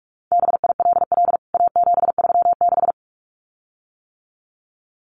Morse code, heard as a keyed tone, about 44 wpm, 700 Hz. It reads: BILL A736
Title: CWT Exchange 45wpm